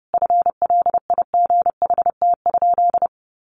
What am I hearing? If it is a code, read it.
Morse code, heard as a keyed tone, about 30 wpm, 700 Hz. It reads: FLIGHT?
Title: Sentences from Top 1000 Words 30wpm